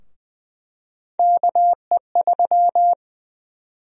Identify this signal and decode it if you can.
Morse code, heard as a keyed tone, about 20 wpm, 700 Hz. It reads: KE3
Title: USA Call Sign Prefixes - Repeated 2x - 20wpm